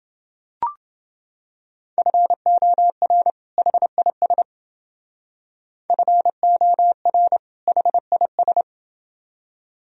Morse code, heard as a keyed tone, about 28 wpm, 700 Hz. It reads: FOR HIS FOR HIS